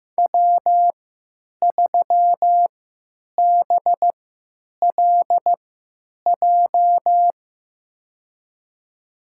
Morse code, heard as a keyed tone, about 15 wpm, spaced as for 9 wpm, 700 Hz. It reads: W3BLJ
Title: USA Call Signs - 5 Characters - 3x Character Spacing - 15wpm